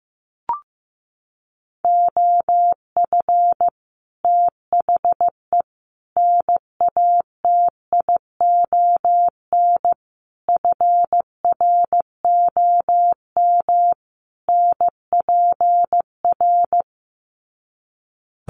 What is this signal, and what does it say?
Morse code, heard as a keyed tone, about 15 wpm, 700 Hz. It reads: OF THE NATION FROM NPR
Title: Sets of 5 Words 15wpm